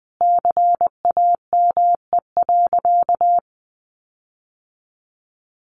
Morse code, heard as a keyed tone, about 19 wpm, 700 Hz. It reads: CAME.
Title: Sentences from Top 1400 Words 20wpm